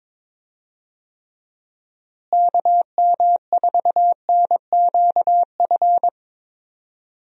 Morse code, heard as a keyed tone, about 22 wpm, 700 Hz. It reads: KM4NQF
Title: USA Call Signs - 6 Characters - Repeated 3x - 22wpm